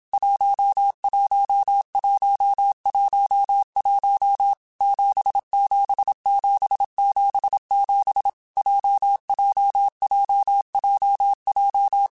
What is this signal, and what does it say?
Morse code, heard as a keyed tone, about 27 wpm, 800 Hz. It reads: 11111 77777 JJJJJ
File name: Groups@25 / Groups@25 mp3s / JZ17@25